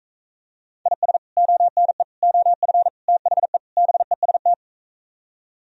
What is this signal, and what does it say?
Morse code, heard as a keyed tone, about 45 wpm, 700 Hz. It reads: IS ONE OF THE BEST